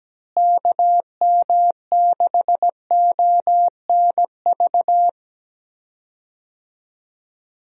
Morse code, heard as a keyed tone, about 17 wpm, 700 Hz. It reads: KM6ONV
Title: USA Call Signs - 6 Characters - Repeated 3x - 17wpm